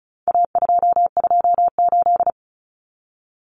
Morse code, heard as a keyed tone, about 34 wpm, 700 Hz. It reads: A228
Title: Common Cut Numbers - 4-Digit 35wpm